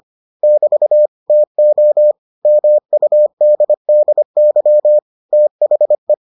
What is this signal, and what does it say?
Morse code, heard as a keyed tone, about 25 wpm, 600 Hz. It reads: <BT> TO MUDDY THE